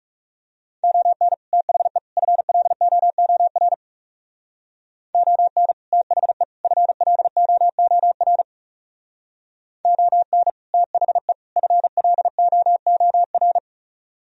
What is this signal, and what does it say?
Morse code, heard as a keyed tone, about 40 wpm, 700 Hz. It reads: ON THE FLOOR ON THE FLOOR ON THE FLOOR